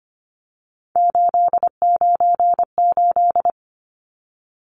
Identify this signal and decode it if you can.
Morse code, heard as a keyed tone, about 25 wpm, 700 Hz. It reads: 898
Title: Numbers - 3-Digit 25wpm